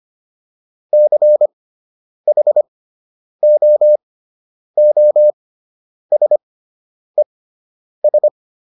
Morse code, heard as a keyed tone, about 25 wpm, spaced as for 9 wpm, 600 Hz. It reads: CHOOSES